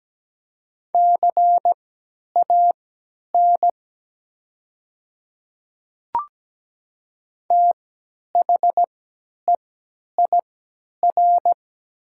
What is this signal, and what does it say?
Morse code, heard as a keyed tone, about 17 wpm, spaced as for 10 wpm, 700 Hz. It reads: CAN THEIR
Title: 3x Character Spacing - Top 100 Words - 17wpm